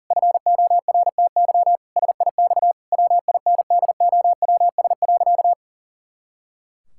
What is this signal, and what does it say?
Morse code, heard as a keyed tone, about 40 wpm, 700 Hz. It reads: FORTY SIX WINDOWS.
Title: Pangram Challenge v2 - 40wpm